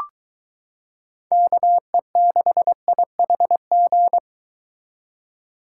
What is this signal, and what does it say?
Morse code, heard as a keyed tone, about 22 wpm, 700 Hz. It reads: KE6IHG